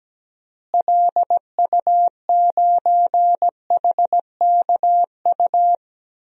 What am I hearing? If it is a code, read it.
Morse code, heard as a keyed tone, about 17 wpm, 700 Hz. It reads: LU9HKU
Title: DXCC-100 Call Signs - 6 Characters - Repeated 3x - 17wpm